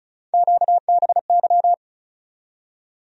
Morse code, heard as a keyed tone, about 36 wpm, 700 Hz. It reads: QBY
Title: USA Call Sign Postfixes - 3 Letters 35wpm